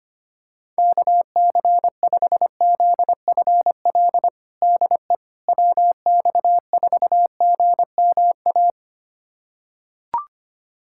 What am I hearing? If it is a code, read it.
Morse code, heard as a keyed tone, about 25 wpm, 700 Hz. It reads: KC5ZFL DE WX4GMA
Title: USA Callsign DE Callsign - 6 Characters - 25wpm